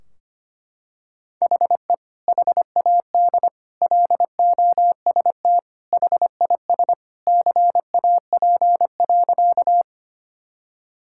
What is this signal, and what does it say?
Morse code, heard as a keyed tone, about 25 wpm, 700 Hz. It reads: HE HAD LOST HIS CAP.
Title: Sentences from Top 1600 Words 25wpm